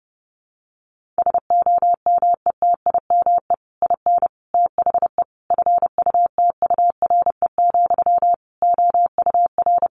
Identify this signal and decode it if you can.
Morse code, heard as a keyed tone, about 30 wpm, 700 Hz. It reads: SOMETIME IN THE FUTURE, OUR
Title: Sentences from Top 1500 Words 30wpm